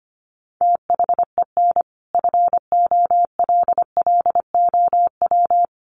Morse code, heard as a keyed tone, about 25 wpm, 700 Hz. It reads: THEN FOLLOW
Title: Sentences from Top 1400 Words 25wpm